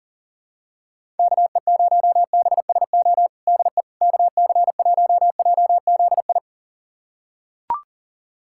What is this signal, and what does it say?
Morse code, heard as a keyed tone, about 40 wpm, 700 Hz. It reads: KE0BSO DE KC1JZI
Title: USA Callsign DE Callsign - 6 Characters - 40wpm